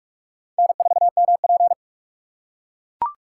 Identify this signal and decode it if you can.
Morse code, heard as a keyed tone, about 45 wpm, 700 Hz. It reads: N4MP
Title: USA Call Signs - 4 Characters - 6x Character Spacing - 45wpm